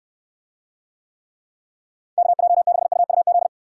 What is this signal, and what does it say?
Morse code, heard as a keyed tone, about 69 wpm, 700 Hz. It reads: KJ6UUZ